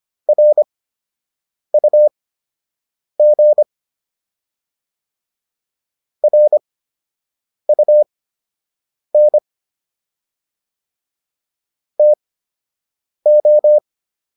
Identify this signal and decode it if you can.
Morse code, heard as a keyed tone, about 25 wpm, spaced as for 7 wpm, 600 Hz. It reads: RUG RUN TO